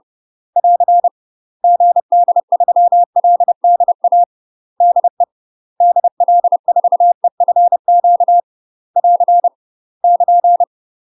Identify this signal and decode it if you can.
Morse code, heard as a keyed tone, about 30 wpm, 700 Hz. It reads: <AR> GD3LDA DE DL4EFQ <AR> <KN>
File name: UK MASTER QSOs TRAINING / 30WPM / UK MASTER QSOs 0019